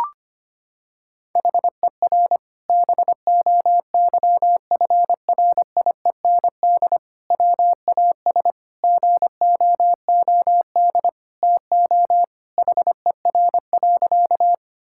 Morse code, heard as a keyed tone, about 25 wpm, 700 Hz. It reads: HER BOYFRIEND WAS GOOD TO HER.
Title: Sentences from Top 1000 Words - Review Part 5 - 25wpm